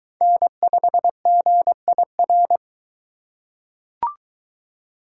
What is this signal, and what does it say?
Morse code, heard as a keyed tone, about 22 wpm, 700 Hz. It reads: N5GIR